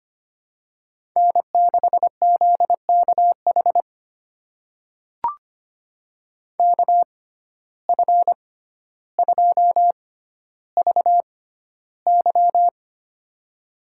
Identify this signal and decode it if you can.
Morse code, heard as a keyed tone, about 25 wpm, spaced as for 13 wpm, 700 Hz. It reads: N6ZKH KF2VY
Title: USA Call Signs - 5 Characters - 6x Character Spacing - 25wpm